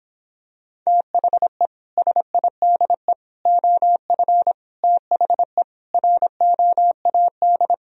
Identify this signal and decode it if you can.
Morse code, heard as a keyed tone, about 26 wpm, 700 Hz. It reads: THE SIDE OF THE ROAD